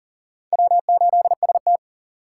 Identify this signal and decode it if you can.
Morse code, heard as a keyed tone, about 50 wpm, 700 Hz. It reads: W8ST